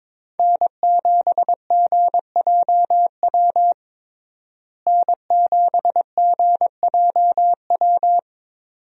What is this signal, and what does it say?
Morse code, heard as a keyed tone, about 22 wpm, 700 Hz. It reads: N7GJW N7GJW
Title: USA Call Signs - 5 Characters - Repeated 2x - 22wpm